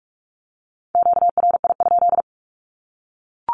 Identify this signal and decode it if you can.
Morse code, heard as a keyed tone, about 44 wpm, 700 Hz. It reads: QRI?
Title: Q-Codes Repeated 3x 45wpm